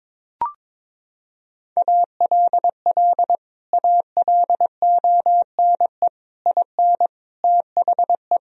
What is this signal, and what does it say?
Morse code, heard as a keyed tone, about 21 wpm, 700 Hz. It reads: ALL ALONE IN THE
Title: Sentences from Top 1000 Words - Review Part 2 - 22wpm